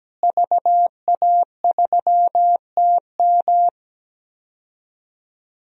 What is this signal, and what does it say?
Morse code, heard as a keyed tone, about 17 wpm, 700 Hz. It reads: VA3TM
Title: DXCC-100 Call Signs - 5 Characters - Repeated 2x - 17wpm